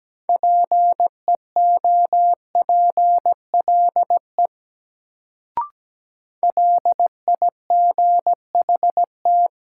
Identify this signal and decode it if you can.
Morse code, heard as a keyed tone, about 17 wpm, 700 Hz. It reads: PEOPLE LIGHT
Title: Sentences from Top 200 Words - New Word Drill - 17wpm